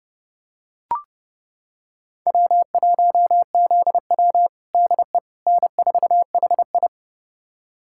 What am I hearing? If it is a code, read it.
Morse code, heard as a keyed tone, about 32 wpm, 700 Hz. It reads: W1ZW DE N4HI
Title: USA Callsign DE Callsign - 4 Characters - 30wpm